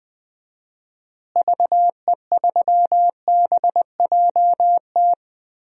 Morse code, heard as a keyed tone, about 20 wpm, 700 Hz. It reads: VE3BJT
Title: DXCC-100 Call Signs - 6 Characters - Repeated 3x - 20wpm